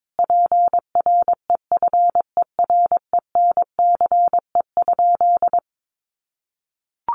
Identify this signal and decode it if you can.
Morse code, heard as a keyed tone, about 22 wpm, 700 Hz. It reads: PREFERENCE?
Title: Sentences from Top 1500 Words 22wpm